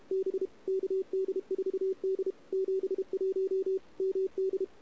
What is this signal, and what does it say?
Morse code, heard as a keyed tone, about 33 wpm, 380 Hz. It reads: B KD4D 71 MD
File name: CW / 2008-ss-cw-k5tr-op-n5rz -01:00